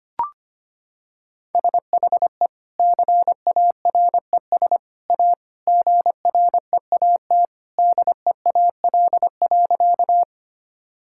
Morse code, heard as a keyed tone, about 25 wpm, 700 Hz. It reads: SHE CARES A GREAT DEAL.
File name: practice / 25wpm / Sentences with 3-Word Phrases - 25wpm